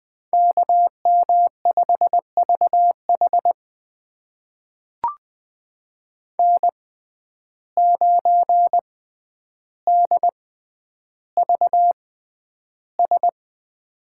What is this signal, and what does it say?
Morse code, heard as a keyed tone, about 20 wpm, spaced as for 10 wpm, 700 Hz. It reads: KM5VH N9DVS
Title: USA Call Signs - 5 Characters - 6x Character Spacing - 20wpm